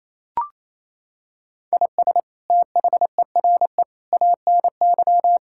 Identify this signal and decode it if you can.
Morse code, heard as a keyed tone, about 28 wpm, 700 Hz. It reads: IS THERE ANY
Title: Sentences from Top 1000 Words - Review Part 6 - 28wpm